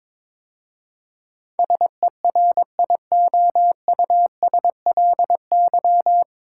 Morse code, heard as a keyed tone, about 23 wpm, 700 Hz. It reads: SERIOUSLY
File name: 9 Letter Words - First 3 Letters Given - 22wpm